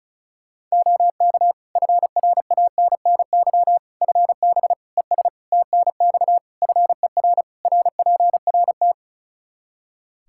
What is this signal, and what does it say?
Morse code, heard as a keyed tone, about 34 wpm, 700 Hz. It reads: OK FRANNY FB ES TNX FER RPRT
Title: Practice QSO - Exchange 2 - Info - 35wpm